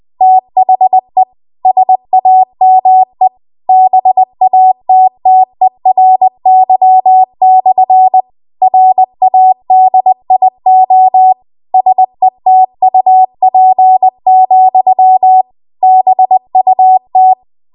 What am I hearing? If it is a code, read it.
Morse code, heard as a keyed tone, about 20 wpm, 750 Hz. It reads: THE SAME BATTERY/ RADIO SETUP, BUT
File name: Archive / 20 WPM / 20WPM